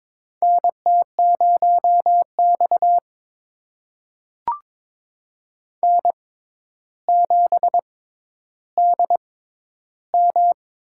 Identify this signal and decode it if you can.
Morse code, heard as a keyed tone, about 23 wpm, spaced as for 11 wpm, 700 Hz. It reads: NT0X N7DM
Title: USA Call Signs - 6x Character Spacing - 22wpm